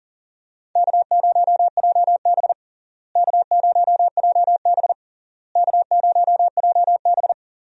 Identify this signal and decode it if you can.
Morse code, heard as a keyed tone, about 40 wpm, 700 Hz. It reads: K0JB K0JB K0JB
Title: USA Call Signs - 4 Characters - Repeated 3x - 40wpm